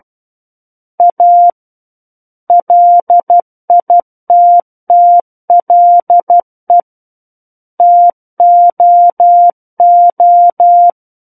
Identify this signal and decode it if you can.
Morse code, heard as a keyed tone, about 12 wpm, 700 Hz. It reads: A LITTLE TOO
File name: Book One / 03-Chapter 3 - 12 WPM